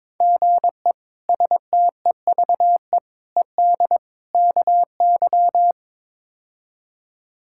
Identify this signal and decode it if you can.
Morse code, heard as a keyed tone, about 22 wpm, 700 Hz. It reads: GE STEVE ED KY
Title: K1USN SST QSO Practice - 22wpm